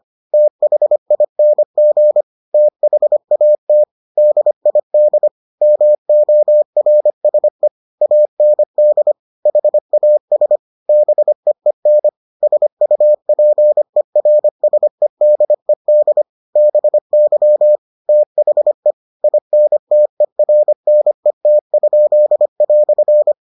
Morse code, heard as a keyed tone, about 25 wpm, 600 Hz. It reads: THING THAT DID MORSE AND HAS BEEN SUPERSEDED BY THE INTERNET?"